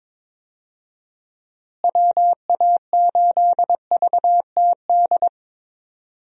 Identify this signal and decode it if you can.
Morse code, heard as a keyed tone, about 22 wpm, 700 Hz. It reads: WA8VTD
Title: USA Call Signs - 6 Characters - Repeated 2x - 22wpm